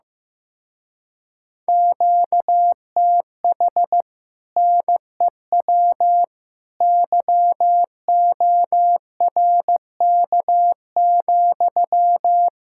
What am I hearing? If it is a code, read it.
Morse code, heard as a keyed tone, about 15 wpm, 700 Hz. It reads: QTH NEW YORK,
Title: Speed-Racing Common QSO Phrases Encore - 15wpm